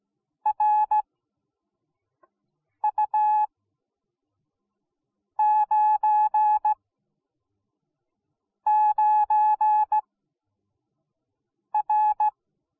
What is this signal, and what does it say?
Morse code, heard as a keyed tone, about 17 wpm, spaced as for 4 wpm, 850 Hz. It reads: RU99R